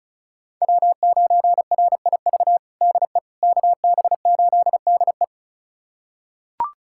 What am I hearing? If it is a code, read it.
Morse code, heard as a keyed tone, about 36 wpm, 700 Hz. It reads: W9RIV DE KB8DE
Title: USA Callsign DE Callsign - 5 Characters - 35wpm